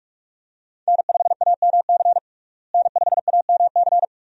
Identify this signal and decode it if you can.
Morse code, heard as a keyed tone, about 43 wpm, 700 Hz. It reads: N5AMC N5AMC
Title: USA Call Signs - 5 Characters - Repeated 2x - 45wpm